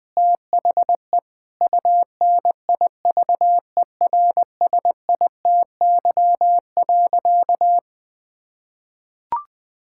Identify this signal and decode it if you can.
Morse code, heard as a keyed tone, about 20 wpm, 700 Hz. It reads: THE UNIVERSITY.
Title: Sentences from Top 1000 Words - Review Part 5 - 20wpm